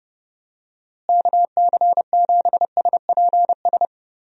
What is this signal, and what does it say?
Morse code, heard as a keyed tone, about 31 wpm, 700 Hz. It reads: KC7SPS
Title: USA Call Signs - 6 Characters - Repeated 3x - 30wpm